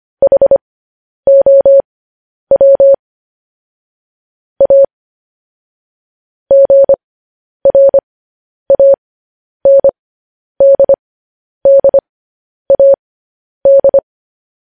Morse code, heard as a keyed tone, about 25 wpm, spaced as for 10 wpm, 550 Hz. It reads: HOW A GRANDDAD
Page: news-headlines-in-morse-code-at-10-wpm